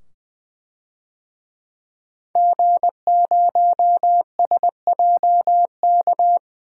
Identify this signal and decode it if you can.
Morse code, heard as a keyed tone, about 20 wpm, 700 Hz. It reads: G0SJK
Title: DXCC-100 Call Signs - 5 Characters - 20wpm